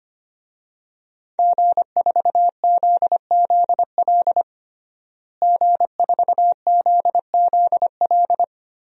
Morse code, heard as a keyed tone, about 25 wpm, 700 Hz. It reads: G4ZZL G4ZZL
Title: DXCC-100 Call Signs - 5 Characters - Repeated 3x - 25wpm